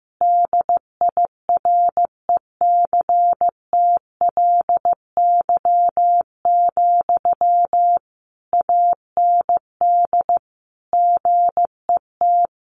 Morse code, heard as a keyed tone, about 15 wpm, 700 Hz. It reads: DIRECTLY, AND GET
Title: Sentences from Top 800 Words 15wpm